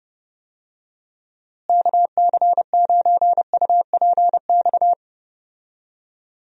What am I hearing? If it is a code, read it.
Morse code, heard as a keyed tone, about 31 wpm, 700 Hz. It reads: KC9UPX
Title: USA Call Signs - 6 Characters - Repeated 2x - 30wpm